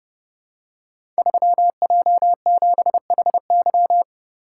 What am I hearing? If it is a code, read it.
Morse code, heard as a keyed tone, about 30 wpm, 700 Hz. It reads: 3J7HY
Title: Code Groups - 5 Letter:Numbers - 30wpm